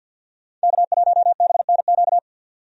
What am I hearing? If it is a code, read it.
Morse code, heard as a keyed tone, about 50 wpm, 700 Hz. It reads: K1BNQ
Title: USA Call Signs - 5 Characters - 50wpm